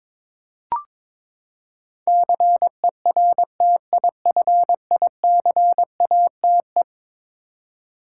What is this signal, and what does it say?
Morse code, heard as a keyed tone, about 22 wpm, 700 Hz. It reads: CERTIFICATE
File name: practice / 22wpm / Eleven Letter Words - 22wpm